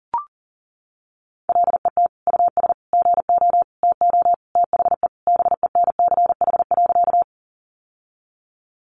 Morse code, heard as a keyed tone, about 40 wpm, 700 Hz. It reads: LET US GO TO THE BENCH.
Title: Sentences from Top 1400 Words 40wpm